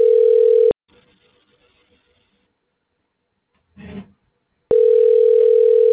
Ringback tone (North America), 440 + 480 Hz.
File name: recordings / RE7b5e d988eecee6efeaf2fa06cd